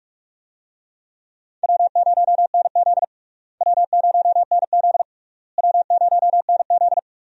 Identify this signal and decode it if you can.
Morse code, heard as a keyed tone, about 46 wpm, 700 Hz. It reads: W0NZ W0NZ W0NZ
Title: USA Call Signs - 4 Characters - Repeated 3x - 45wpm